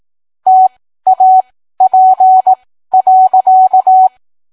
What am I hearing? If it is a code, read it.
Morse code, heard as a keyed tone, about 18 wpm, spaced as for 13 wpm, 750 Hz. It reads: TAP.